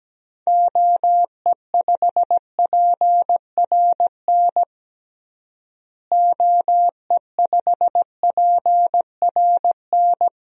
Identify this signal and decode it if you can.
Morse code, heard as a keyed tone, about 17 wpm, 700 Hz. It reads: OE5PRN OE5PRN
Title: DXCC-100 Call Signs - 6 Characters - Repeated 3x - 17wpm